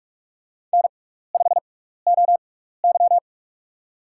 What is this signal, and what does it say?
Morse code, heard as a keyed tone, about 46 wpm, spaced as for 16 wpm, 700 Hz. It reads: N5OY
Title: USA Call Signs - 4 Characters - 6x Character Spacing - 45wpm